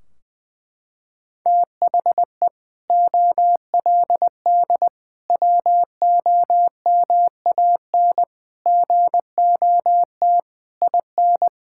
Morse code, heard as a keyed tone, about 19 wpm, 700 Hz. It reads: THE OLD WOMAN GOT IN
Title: Sentences from Top 1600 Words 20wpm